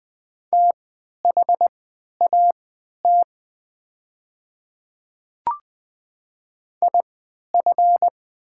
Morse code, heard as a keyed tone, about 20 wpm, spaced as for 11 wpm, 690 Hz. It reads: THAT IF
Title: 3x Character Spacing - Top 100 Words - 20wpm